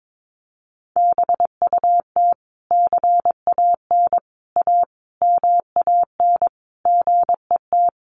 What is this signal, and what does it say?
Morse code, heard as a keyed tone, about 22 wpm, 700 Hz. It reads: BUT CAN A MAN GET
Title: Sentences from Top 1400 Words 22wpm